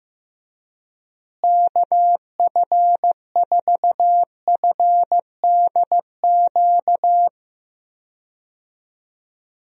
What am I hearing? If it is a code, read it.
Morse code, heard as a keyed tone, about 15 wpm, 700 Hz. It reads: KF4FDQ
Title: USA Call Signs - 6 Characters - Repeated 3x - 15wpm